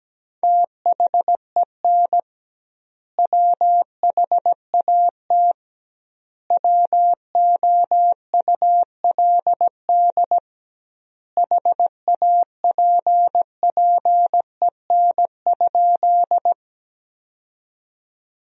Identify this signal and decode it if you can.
Morse code, heard as a keyed tone, about 17 wpm, 700 Hz. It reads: THEN WHAT WOULD HAPPEN?
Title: 2x Word-Spacing - Sentences from Top 400 Words - 17wpm